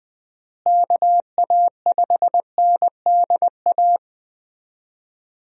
Morse code, heard as a keyed tone, about 20 wpm, 700 Hz. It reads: KA5NDA
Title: USA Call Signs - 6 Characters - Repeated 3x - 20wpm